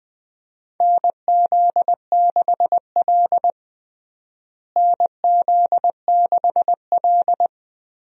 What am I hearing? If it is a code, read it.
Morse code, heard as a keyed tone, about 21 wpm, 700 Hz. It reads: NZ6L NZ6L
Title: USA Call Signs - 4 Characters - Repeated 2x - 20wpm